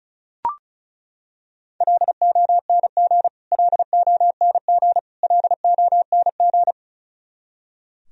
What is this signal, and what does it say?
Morse code, heard as a keyed tone, about 34 wpm, 700 Hz. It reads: LONG LONG LONG
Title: Top 100 Words Repeated 35wpm